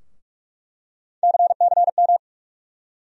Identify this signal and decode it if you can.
Morse code, heard as a keyed tone, about 46 wpm, 700 Hz. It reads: CCM
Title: USA Call Sign Postfixes - 3 Letters 45wpm